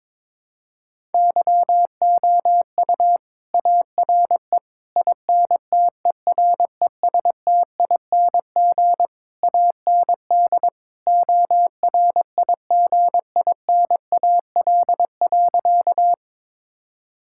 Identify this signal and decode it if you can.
Morse code, heard as a keyed tone, about 21 wpm, 700 Hz. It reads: YOU ARE INTERESTING AND ORIGINAL.
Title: Sentences from Top 1000 Words - Review Part 6 - 22wpm